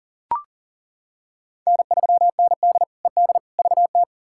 Morse code, heard as a keyed tone, about 40 wpm, 700 Hz. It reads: N3ND ED VT
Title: K1USN SST QSO Practice - 40wpm